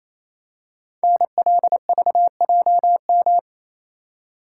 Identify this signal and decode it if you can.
Morse code, heard as a keyed tone, about 27 wpm, 690 Hz. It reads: NLVJM
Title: Code Groups - 5 Letters - 28wpm